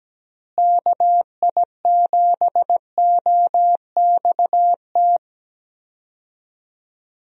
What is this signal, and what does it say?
Morse code, heard as a keyed tone, about 17 wpm, 700 Hz. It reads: KI7OXT
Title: USA Call Signs - 6 Characters - Repeated 2x - 17wpm